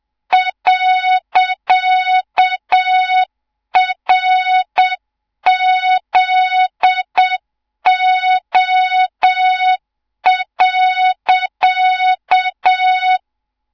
Morse code, heard as a keyed tone, about 7 wpm, 3700 Hz, with noise. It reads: .RZO.
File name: Morze-HA2MN / hang / 24 GYAKORLAT